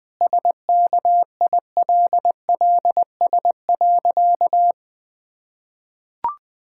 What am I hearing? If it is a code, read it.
Morse code, heard as a keyed tone, about 20 wpm, 700 Hz. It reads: SKILLS.
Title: Sentences with 2-Word Phrases - 20wpm